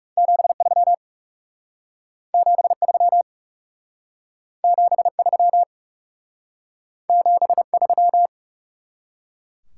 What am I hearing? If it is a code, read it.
Morse code, heard as a keyed tone, about 38 wpm, 700 Hz. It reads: 73 73 73 73
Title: Speed-Racing Top 90 QSO Elements 30wpm